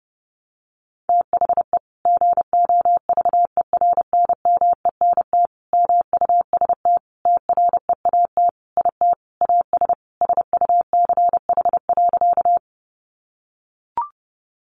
Morse code, heard as a keyed tone, about 28 wpm, 700 Hz. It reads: THE GOVERNMENT MUST TREAT IT AS SUCH.
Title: Sentences from Top 1300 Words 30wpm